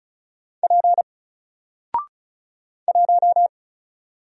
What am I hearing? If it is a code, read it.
Morse code, heard as a keyed tone, about 36 wpm, 700 Hz. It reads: P 1
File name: Single-Letter-Number-35wpm